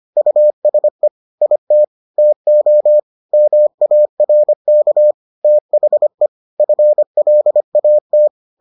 Morse code, heard as a keyed tone, about 25 wpm, 600 Hz. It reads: USE IT TO MARK THE FLAT